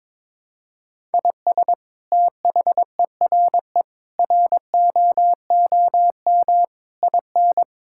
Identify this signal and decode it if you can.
Morse code, heard as a keyed tone, about 22 wpm, 700 Hz. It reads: IS THERE ROOM IN
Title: Sentences from Top 1000 Words - Review Part 5 - 22wpm